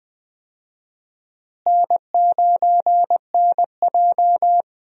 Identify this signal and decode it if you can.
Morse code, heard as a keyed tone, about 20 wpm, 700 Hz. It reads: N9NJ
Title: USA Call Signs - 4 Characters - Repeated 2x - 20wpm